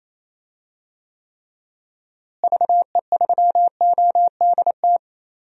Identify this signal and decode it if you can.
Morse code, heard as a keyed tone, about 28 wpm, 700 Hz. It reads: VE3ODT
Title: DXCC-100 Call Signs - 6 Characters - Repeated 3x - 28wpm